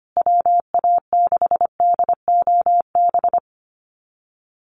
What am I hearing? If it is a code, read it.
Morse code, heard as a keyed tone, about 25 wpm, 700 Hz. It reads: WA6DOB
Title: USA Call Signs Repeated 2x 25wpm